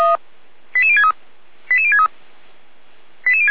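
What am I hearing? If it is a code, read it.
Five-tone selective call (ZVEI), heard as quick runs of five tones, tones of about 80 ms. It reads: 88063 88063
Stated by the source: ZVEI five-tone selective call, code 88391, received via SDR.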